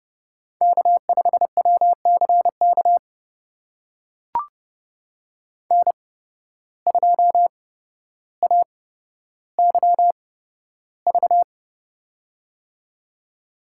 Morse code, heard as a keyed tone, about 30 wpm, spaced as for 13 wpm, 700 Hz. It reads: K5WCK N2AYV
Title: USA Call Signs - 8x Character Spacing - 30wpm